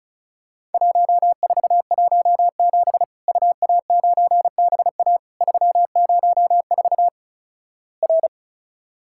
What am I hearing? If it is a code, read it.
Morse code, heard as a keyed tone, about 36 wpm, 700 Hz. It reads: 1417 UA9BA 304 R
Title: Worked All Europe DX Contest - QTC Exchange - 35wpm